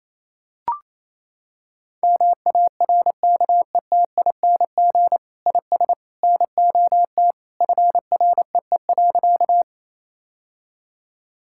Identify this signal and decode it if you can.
Morse code, heard as a keyed tone, about 27 wpm, 700 Hz. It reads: MARKETING IS NOT FREE.
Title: Sentences from Top 1000 Words - Review Part 5 - 28wpm